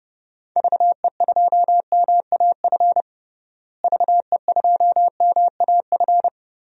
Morse code, heard as a keyed tone, about 30 wpm, 700 Hz. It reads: VE2MAF VE2MAF
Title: DXCC-100 Call Signs - 6 Characters - Repeated 3x - 30wpm